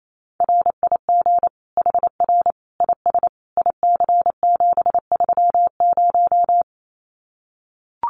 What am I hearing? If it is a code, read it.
Morse code, heard as a keyed tone, about 28 wpm, 700 Hz. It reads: RIG HR IS IC730
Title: Common QSO Phrases 28wpm